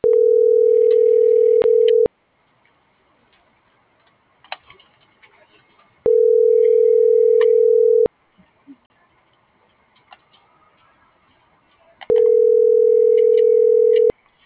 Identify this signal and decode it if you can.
Ringback tone (North America), 440 + 480 Hz.